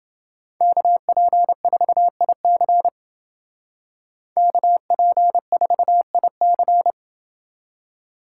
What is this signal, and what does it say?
Morse code, heard as a keyed tone, about 29 wpm, 700 Hz. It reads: KP4IC KP4IC